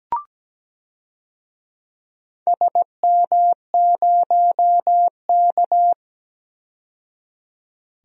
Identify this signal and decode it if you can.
Morse code, heard as a keyed tone, about 17 wpm, 700 Hz. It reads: SM0K
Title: DXCC-100 Call Signs - 4 Characters - 17wpm